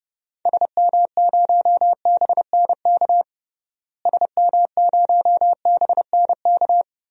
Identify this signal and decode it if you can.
Morse code, heard as a keyed tone, about 32 wpm, 700 Hz. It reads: SM0BNK SM0BNK
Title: DXCC-100 Call Signs - 6 Characters - Repeated 3x - 30wpm